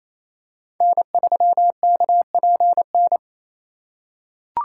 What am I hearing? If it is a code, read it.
Morse code, heard as a keyed tone, about 28 wpm, 700 Hz. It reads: N3KPN
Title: USA Call Signs - 5 Characters - 4x Character Spacing - 28wpm